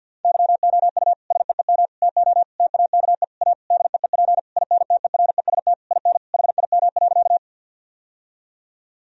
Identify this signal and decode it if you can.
Morse code, heard as a keyed tone, about 50 wpm, 700 Hz. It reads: YOU SEEM TO TAKE A DEEP INTEREST IN HIM.
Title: Sentences from Top 1000 Words - Review Part 3 - 50wpm